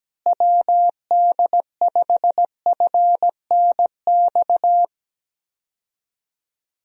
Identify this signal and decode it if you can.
Morse code, heard as a keyed tone, about 17 wpm, 700 Hz. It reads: WD5FNX
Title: USA Call Signs - 6 Characters - Repeated 2x - 17wpm